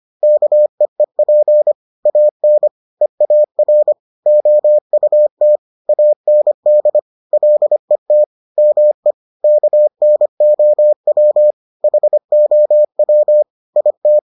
Morse code, heard as a keyed tone, about 25 wpm, 600 Hz. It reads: KEEP AN EAR OUT AND LET ME KNOW HOW IT